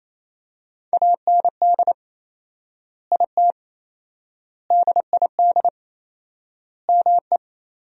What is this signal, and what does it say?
Morse code, heard as a keyed tone, about 28 wpm, 700 Hz. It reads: AND IT DID ME
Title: 4x Word-Spacing - Sentences from Top 200 Words - 28wpm